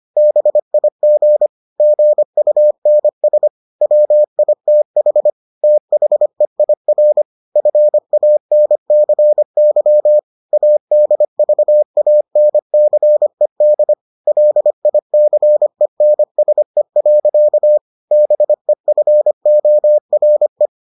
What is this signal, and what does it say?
Morse code, heard as a keyed tone, about 25 wpm, 600 Hz. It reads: BIG GUNS WITH THEIR FANCY ADVANCED LICENSE. BEFORE